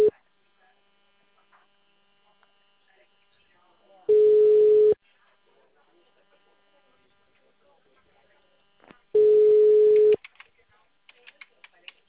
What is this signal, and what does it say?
Ringback tone (Europe, 425 Hz), with speech.